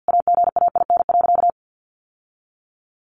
Morse code, heard as a keyed tone, about 50 wpm, 700 Hz. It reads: AGAIN.
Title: Sentences from Top 1300 Words 50wpm